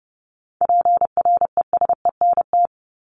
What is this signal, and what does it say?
Morse code, heard as a keyed tone, about 30 wpm, 700 Hz. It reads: PRESENT
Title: Top 501-750 Words 30wpm